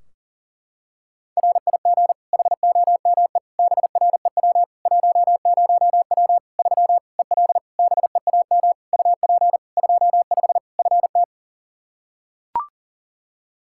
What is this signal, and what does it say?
Morse code, heard as a keyed tone, about 40 wpm, 700 Hz. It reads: RIG HOME BREW 10W 3 EL BEAM UP 25 FT
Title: K1USN SST QSO Practice 40wpm